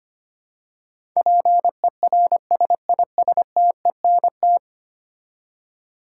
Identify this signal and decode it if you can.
Morse code, heard as a keyed tone, about 25 wpm, 700 Hz. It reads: PERSISTENT
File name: 10 Letter Words - First 3 Letters Given - 25wpm